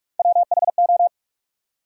Morse code, heard as a keyed tone, about 59 wpm, 700 Hz. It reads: WHO